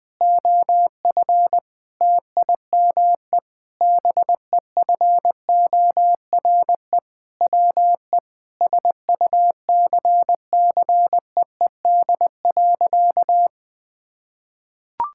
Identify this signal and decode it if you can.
Morse code, heard as a keyed tone, about 20 wpm, 700 Hz. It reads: OF TIME BEFORE WE SUCCEED.
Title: Sentences with 6-Word Phrases - 20wpm